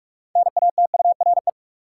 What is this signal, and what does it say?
Morse code, heard as a keyed tone, about 46 wpm, 700 Hz. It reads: NATURE
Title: Sentences from Top 600 Words - New Word Drill - 45wpm